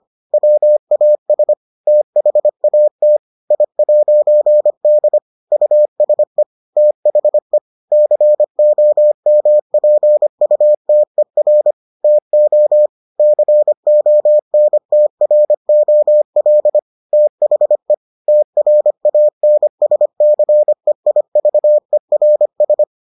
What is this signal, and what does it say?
Morse code, heard as a keyed tone, about 25 wpm, 600 Hz. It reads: WAS THAT I'D USE THE COMPUTER TO CONTROL THE TRANSCEIVERS